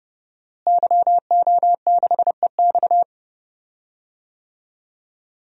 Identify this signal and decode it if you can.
Morse code, heard as a keyed tone, about 29 wpm, 700 Hz. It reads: YO6EX
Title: DXCC-100 Call Signs - 5 Characters - Repeated 3x - 30wpm